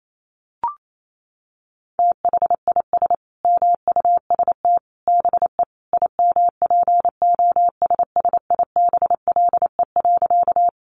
Morse code, heard as a keyed tone, about 28 wpm, 700 Hz. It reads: THIS MUST BE IMPOSSIBLE.
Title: Sentences from Top 1100 Words 28wpm